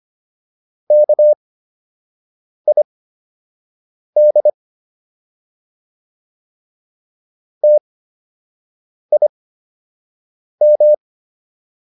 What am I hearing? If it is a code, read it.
Morse code, heard as a keyed tone, about 25 wpm, spaced as for 6 wpm, 600 Hz. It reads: KID TIM